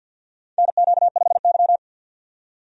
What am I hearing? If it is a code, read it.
Morse code, heard as a keyed tone, about 51 wpm, 700 Hz. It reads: NQ5Y
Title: USA Call Signs - 4 Characters - 50wpm